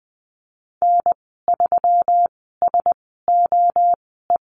Morse code, heard as a keyed tone, about 20 wpm, spaced as for 15 wpm, 700 Hz. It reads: N3SOE
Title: 122 Lesson 20g - Number 3 - Call Signs 20wpm